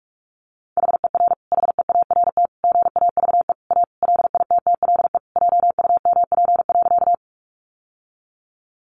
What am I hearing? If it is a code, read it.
Morse code, heard as a keyed tone, about 44 wpm, 700 Hz. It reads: HER HEART GAVE A LITTLE JUMP.
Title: Sentences from Top 1300 Words 45wpm